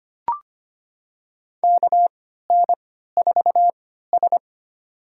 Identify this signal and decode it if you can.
Morse code, heard as a keyed tone, about 25 wpm, spaced as for 14 wpm, 700 Hz. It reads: KN4S
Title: USA Call Signs - 4 Characters - 3x Character Spacing - 25wpm